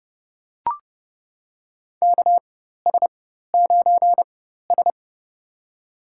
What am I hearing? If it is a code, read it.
Morse code, heard as a keyed tone, about 31 wpm, spaced as for 14 wpm, 700 Hz. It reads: KS9S
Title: USA Call Signs - 4 Characters - 4x Character Spacing - 30wpm